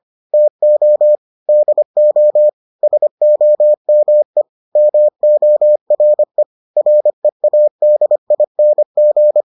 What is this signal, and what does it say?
Morse code, heard as a keyed tone, about 25 wpm, 600 Hz. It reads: TO DO SOME MORE READING